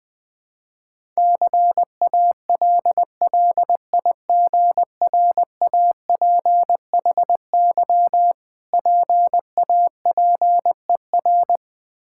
Morse code, heard as a keyed tone, about 20 wpm, 690 Hz. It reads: CALLIGRAPHY PAPER
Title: Difficult Words with Hints - 20wpm